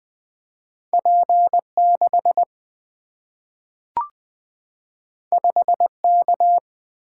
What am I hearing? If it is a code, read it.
Morse code, heard as a keyed tone, about 20 wpm, 700 Hz. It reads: P 6 5 K
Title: Single-Letter Single-Number 20wpm